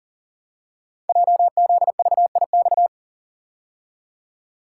Morse code, heard as a keyed tone, about 40 wpm, 700 Hz. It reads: JZVIX
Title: Code Groups - 5 Letters - 40wpm